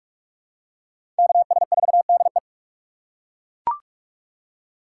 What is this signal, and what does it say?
Morse code, heard as a keyed tone, about 41 wpm, 700 Hz. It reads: KS4DE
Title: USA Call Signs - 5 Characters - 6x Character Spacing - 45wpm